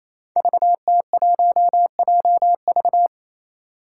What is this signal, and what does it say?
Morse code, heard as a keyed tone, about 28 wpm, 700 Hz. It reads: VT1JV
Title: Five Challenging Letter-Number Combinations 28wpm